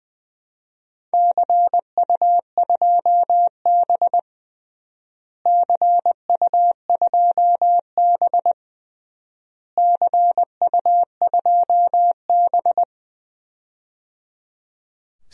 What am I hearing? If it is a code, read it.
Morse code, heard as a keyed tone, about 20 wpm, 700 Hz. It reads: CU2B CU2B CU2B
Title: DXCC-100 Call Signs - 4 Characters - Repeated 3x - 20wpm